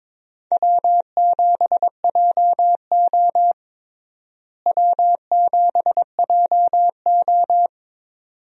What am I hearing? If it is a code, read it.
Morse code, heard as a keyed tone, about 22 wpm, 700 Hz. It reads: W7JO W7JO
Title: USA Call Signs - 4 Characters - Repeated 2x - 22wpm